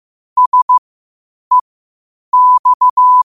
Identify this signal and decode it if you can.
Morse code, heard as a keyed tone, about 15 wpm, spaced as for 8 wpm, 1000 Hz. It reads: SEX
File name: qotdfiles / 8wpm